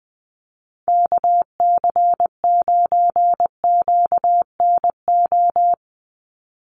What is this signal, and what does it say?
Morse code, heard as a keyed tone, about 20 wpm, 700 Hz. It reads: KC9QNO
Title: USA Call Signs 20wpm